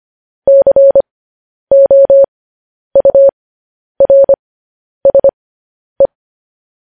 Morse code, heard as a keyed tone, about 25 wpm, spaced as for 10 wpm, 550 Hz. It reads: COURSE
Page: News Headlines in Morse Code at 10 WPM